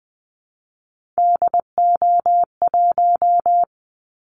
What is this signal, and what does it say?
Morse code, heard as a keyed tone, about 20 wpm, 700 Hz. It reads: DO1
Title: Callsign Prefixes for Top 50 DXCC - 20wpm